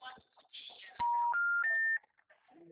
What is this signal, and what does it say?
Special information tone, with speech.